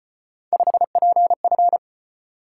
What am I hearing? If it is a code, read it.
Morse code, heard as a keyed tone, about 33 wpm, 700 Hz. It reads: 5PF